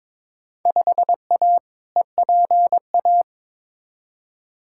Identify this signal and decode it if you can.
Morse code, heard as a keyed tone, about 22 wpm, 700 Hz. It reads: 5A EPA
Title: Field Day Exchanges - 22wpm